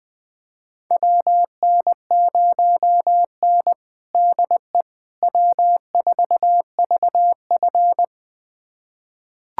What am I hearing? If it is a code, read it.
Morse code, heard as a keyed tone, about 21 wpm, 700 Hz. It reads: WN0N DE W4VF
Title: USA Callsign DE Callsign - 4 Characters - 20wpm